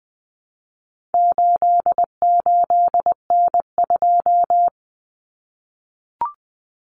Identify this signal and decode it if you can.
Morse code, heard as a keyed tone, about 20 wpm, 700 Hz. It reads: 88N2
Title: Common Cut Numbers - 4-Digit 20wpm